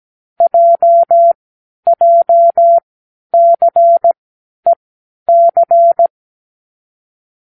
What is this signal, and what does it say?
Morse code, heard as a keyed tone, about 17 wpm, spaced as for 10 wpm, 680 Hz. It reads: JJCEC
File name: k7qo / Lesson15(A-J)